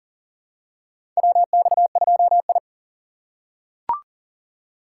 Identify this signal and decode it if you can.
Morse code, heard as a keyed tone, about 41 wpm, 700 Hz. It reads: WX2I
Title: USA Call Signs - 4 Characters - 3x Character Spacing - 40wpm